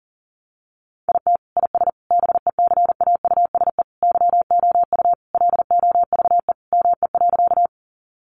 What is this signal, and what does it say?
Morse code, heard as a keyed tone, about 40 wpm, 700 Hz. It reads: IT IS BECAUSE YOU LOVE ME.
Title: Sentences from Top 600 Words 40wpm